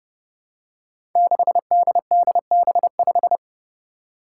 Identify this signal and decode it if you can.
Morse code, heard as a keyed tone, about 32 wpm, 700 Hz. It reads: 6DDB5
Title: Five Challenging Letter-Number Combinations 30wpm